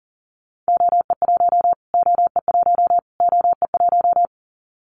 Morse code, heard as a keyed tone, about 40 wpm, 700 Hz. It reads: OE1 OE1 OE1
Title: Callsign Prefixes for Top 25 DXCC - Repeated 3x - 40wpm